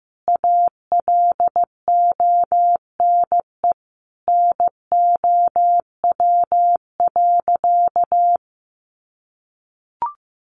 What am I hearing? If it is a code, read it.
Morse code, heard as a keyed tone, about 15 wpm, 700 Hz. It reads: ALONE NOW.
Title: Sentences from Top 1100 Words 15wpm